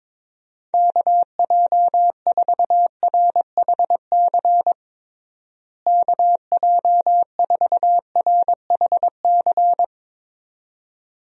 Morse code, heard as a keyed tone, about 22 wpm, 700 Hz. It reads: KJ4RHC KJ4RHC
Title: USA Call Signs - 6 Characters - Repeated 2x - 22wpm